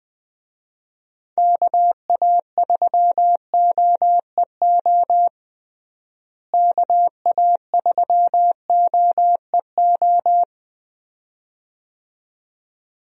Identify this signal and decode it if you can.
Morse code, heard as a keyed tone, about 20 wpm, 700 Hz. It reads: KA3OEO KA3OEO
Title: USA Call Signs - 6 Characters - Repeated 2x - 20wpm